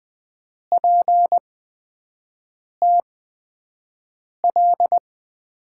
Morse code, heard as a keyed tone, about 20 wpm, spaced as for 5 wpm, 700 Hz. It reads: PTL
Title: USA Call Signs - 5 Characters - 8x Character Spacing - 20wpm